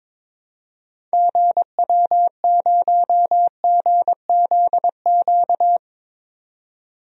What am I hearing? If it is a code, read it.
Morse code, heard as a keyed tone, about 22 wpm, 700 Hz. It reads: GW0GZQ
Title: DXCC-100 Call Signs - 6 Characters - 22wpm